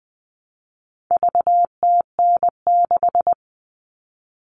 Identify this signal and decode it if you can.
Morse code, heard as a keyed tone, about 20 wpm, 700 Hz. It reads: VTN6
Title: All Cut Numbers - 4-Digit 20wpm